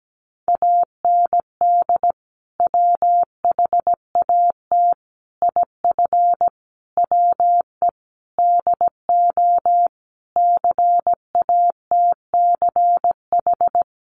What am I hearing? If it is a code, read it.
Morse code, heard as a keyed tone, about 17 wpm, 700 Hz. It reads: AND WHAT IF WE DO CATCH
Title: Sentences from Top 1500 Words 17wpm